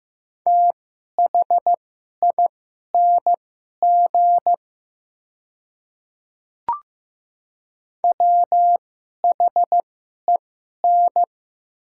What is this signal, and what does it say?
Morse code, heard as a keyed tone, about 16 wpm, spaced as for 11 wpm, 700 Hz. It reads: THING WHEN
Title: 2x Character Spacing - Top 200 Words - 15wpm